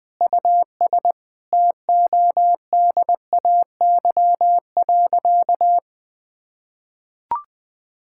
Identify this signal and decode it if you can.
Morse code, heard as a keyed tone, about 20 wpm, 700 Hz. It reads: US TODAY.
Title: Sentences with 5-Word Phrases - 20wpm